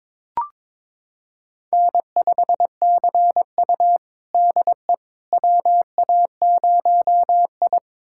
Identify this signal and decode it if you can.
Morse code, heard as a keyed tone, about 22 wpm, 700 Hz. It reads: N5CU DE WA0I
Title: USA Callsign DE Callsign - 4 Characters - 22wpm